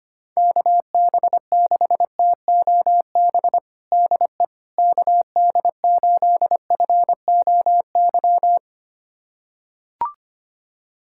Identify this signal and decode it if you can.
Morse code, heard as a keyed tone, about 25 wpm, 700 Hz. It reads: KB6TOB DE KD8FOY
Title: USA Callsign DE Callsign 25wpm